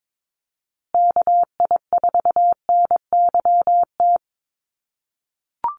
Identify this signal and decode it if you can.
Morse code, heard as a keyed tone, about 22 wpm, 700 Hz. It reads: KI4NYT
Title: USA Call Signs Repeated 2x 22wpm